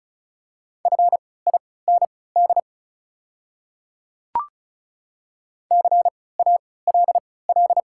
Morse code, heard as a keyed tone, about 34 wpm, spaced as for 20 wpm, 690 Hz. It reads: FIND CALL
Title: 3x Character Spacing - Top 100 Words - 35wpm